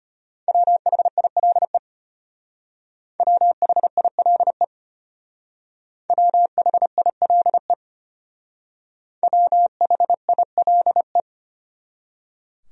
Morse code, heard as a keyed tone, about 31 wpm, 700 Hz. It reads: WHILE WHILE WHILE WHILE
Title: Speed-Racing Top 500 Words 25wpm